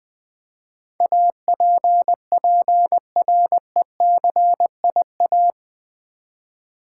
Morse code, heard as a keyed tone, about 21 wpm, 700 Hz. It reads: APPRECIA
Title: Build A Word - 4 Letters At A Time - 12-Letter Words - 20wpm